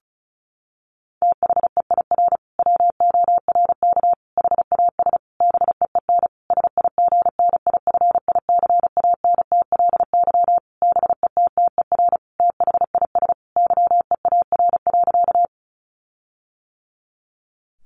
Morse code, heard as a keyed tone, about 34 wpm, 700 Hz. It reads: THEIR WORK HAS BEEN SIGNIFICANTLY BETTER THIS YEAR.
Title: Sentences from Top 1500 Words 35wpm